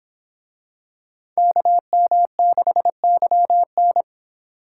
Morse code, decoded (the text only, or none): KM6YN